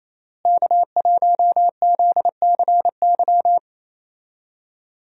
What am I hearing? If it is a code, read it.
Morse code, heard as a keyed tone, about 28 wpm, 700 Hz. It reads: K1ZCY
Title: USA Call Signs - 5 Characters - Repeated 3x - 28wpm